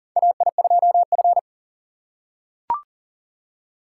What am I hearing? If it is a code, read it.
Morse code, heard as a keyed tone, about 40 wpm, 700 Hz. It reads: AI2F
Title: USA Call Signs - 4 Characters - 3x Character Spacing - 40wpm